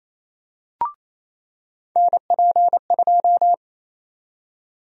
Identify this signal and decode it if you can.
Morse code, heard as a keyed tone, about 28 wpm, 700 Hz. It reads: NP2
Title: USA Call Sign Prefixes - 28wpm